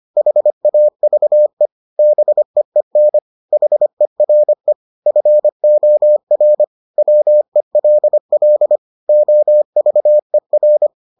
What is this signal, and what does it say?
Morse code, heard as a keyed tone, about 25 wpm, 600 Hz. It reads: HAVE BEEN HERE FOR WELL OVER